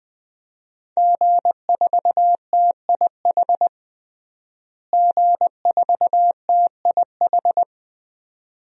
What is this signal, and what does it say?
Morse code, heard as a keyed tone, about 20 wpm, 700 Hz. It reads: G4TIH G4TIH
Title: DXCC-100 Call Signs - 5 Characters - Repeated 2x - 20wpm